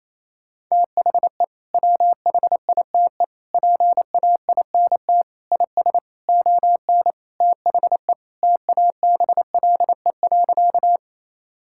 Morse code, heard as a keyed tone, about 28 wpm, 700 Hz. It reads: THE WHITE PAINT IS ON THE TABLE.
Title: Sentences from Top 1000 Words - Review Part 6 - 28wpm